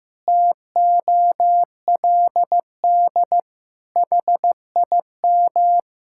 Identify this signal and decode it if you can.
Morse code, heard as a keyed tone, about 15 wpm, 700 Hz. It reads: TOLD HIM
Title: Pangram Challenge v2 - 15wpm